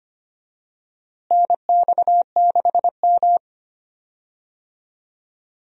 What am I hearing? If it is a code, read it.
Morse code, heard as a keyed tone, about 25 wpm, 700 Hz. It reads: NX6M
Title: USA Call Signs - 4 Characters - Repeated 2x - 25wpm